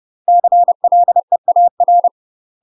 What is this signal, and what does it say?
Morse code, heard as a keyed tone, about 30 wpm, 700 Hz. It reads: CLEAR
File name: UK MASTER QSOs TRAINING / 30WPM / UK MASTER QSOs 0019